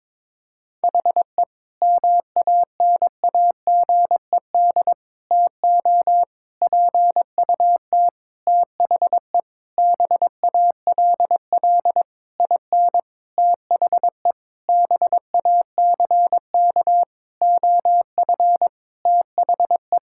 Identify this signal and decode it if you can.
Morse code, heard as a keyed tone, about 22 wpm, 700 Hz. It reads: HE MANAGED TO PUT THE BALL IN THE BACK OF THE
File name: Sentences with 9-Word Phrases - 22wpm